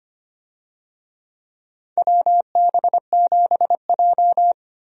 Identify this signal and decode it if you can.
Morse code, heard as a keyed tone, about 25 wpm, 700 Hz. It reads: WB7J
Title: USA Call Signs - 4 Characters - 25wpm